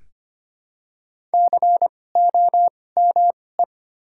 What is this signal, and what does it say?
Morse code, heard as a keyed tone, about 25 wpm, spaced as for 18 wpm, 700 Hz. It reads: COME
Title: 2x Character Spacing - Top 100 Words - 25wpm